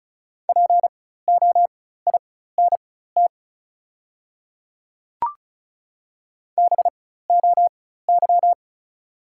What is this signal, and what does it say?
Morse code, heard as a keyed tone, about 36 wpm, spaced as for 16 wpm, 690 Hz. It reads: POINT BOY
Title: 4x Character Spacing - Top 200 Words - 35wpm